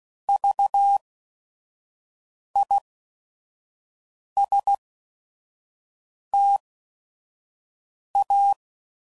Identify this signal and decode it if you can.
Morse code, heard as a keyed tone, about 16 wpm, spaced as for 5 wpm, 800 Hz. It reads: VISTA
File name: Prefixes / By First Letter / Vs